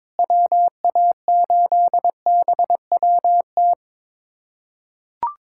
Morse code, heard as a keyed tone, about 22 wpm, 700 Hz. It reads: WA8BWT
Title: USA Callsign DE Callsign 22wpm